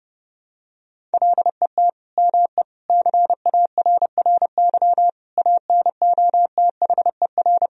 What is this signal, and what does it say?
Morse code, heard as a keyed tone, about 30 wpm, 700 Hz. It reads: LET ME CARRY ANOTHER
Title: Sentences from Top 1000 Words - Review Part 1 - 30wpm